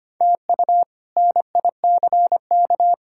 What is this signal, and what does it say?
Morse code, heard as a keyed tone, about 25 wpm, 700 Hz. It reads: TU NICK
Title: K1USN SST QSO Practice - 25wpm